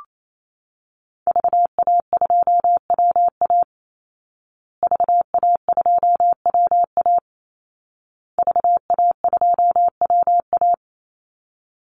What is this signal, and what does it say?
Morse code, heard as a keyed tone, about 28 wpm, 700 Hz. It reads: VA2WA VA2WA VA2WA
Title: Call Signs Repeated 2x - 28wpm